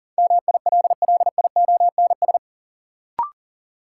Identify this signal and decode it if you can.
Morse code, heard as a keyed tone, about 40 wpm, 700 Hz. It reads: MILLIONS
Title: Sets of 5 Words - New Word Drill - 40wpm